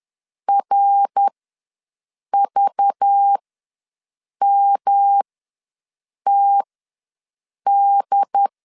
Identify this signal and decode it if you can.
Morse code, heard as a keyed tone, about 10 wpm, spaced as for 6 wpm, 800 Hz. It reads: RVMTD
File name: LEZIONI / 5 LBVC / 5 LBV ws 10wpm 3645